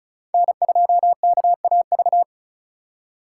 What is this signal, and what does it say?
Morse code, heard as a keyed tone, about 36 wpm, 700 Hz. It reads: N2KAV
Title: USA Call Signs - 5 Characters - 4x Character Spacing - 35wpm